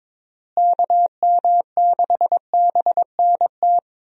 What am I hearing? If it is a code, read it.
Morse code, heard as a keyed tone, about 22 wpm, 700 Hz. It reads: KM6BNT
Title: USA Call Signs - 6 Characters - Repeated 2x - 22wpm